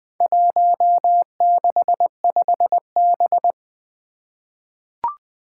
Morse code, heard as a keyed tone, about 21 wpm, 700 Hz. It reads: 165B
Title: Four Challenging Letter-Number Combinations 20wpm